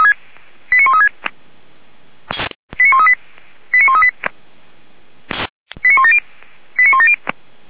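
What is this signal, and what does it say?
Five-tone selective call (ZVEI), heard as quick runs of five tones, tones of about 70 ms. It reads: 89136 89138 89138 89170 89170
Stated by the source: ZVEI five-tone fire brigade alerting received via SDR.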